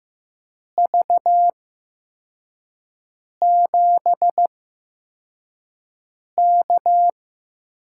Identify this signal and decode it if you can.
Morse code, heard as a keyed tone, about 15 wpm, spaced as for 4 wpm, 700 Hz. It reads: V7K
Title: USA Call Signs - 4 Characters - 8x Character Spacing - 15wpm